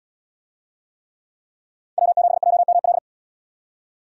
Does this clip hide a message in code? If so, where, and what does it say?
Yes: Morse code from 2.0 to 3.0 s, heard as a keyed tone, about 73 wpm, 700 Hz. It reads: W7JML